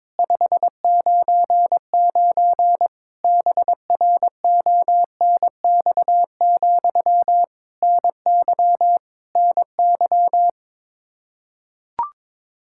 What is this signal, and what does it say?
Morse code, heard as a keyed tone, about 22 wpm, 700 Hz. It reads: 599 BRONX, NY NY
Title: Practice QSO - Exchange 1 - Beginning - 22wpm